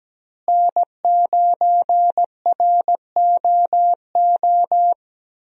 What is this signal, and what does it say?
Morse code, heard as a keyed tone, about 17 wpm, 700 Hz. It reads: N9ROO
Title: USA Call Signs - 5 Characters - Repeated 3x - 17wpm